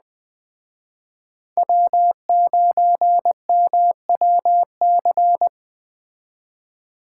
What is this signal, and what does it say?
Morse code, heard as a keyed tone, about 21 wpm, 700 Hz. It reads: W9MWC